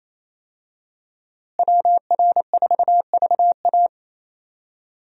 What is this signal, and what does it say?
Morse code, heard as a keyed tone, about 28 wpm, 700 Hz. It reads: WR4VA
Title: USA Call Signs - 5 Characters - Repeated 2x - 28wpm